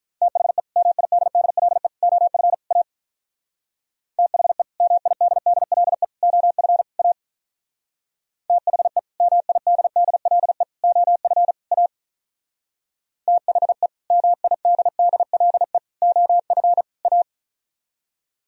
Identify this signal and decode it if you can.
Morse code, heard as a keyed tone, about 44 wpm, 700 Hz. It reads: THE MIDDLE OF A THE MIDDLE OF A THE MIDDLE OF A THE MIDDLE OF A
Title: Speed-Racing - Sets of 4 Words 35wpm